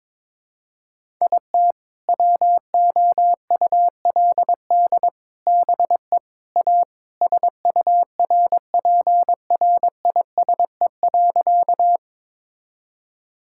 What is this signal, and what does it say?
Morse code, heard as a keyed tone, about 22 wpm, 700 Hz. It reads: IT WOULD BE A SURPRISE.
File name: Sentences with 4-Word Phrases - 22wpm